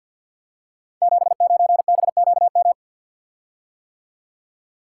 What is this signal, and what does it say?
Morse code, heard as a keyed tone, about 51 wpm, 700 Hz. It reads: 79BQM
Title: Code Groups - 5 Letter:Numbers - 50wpm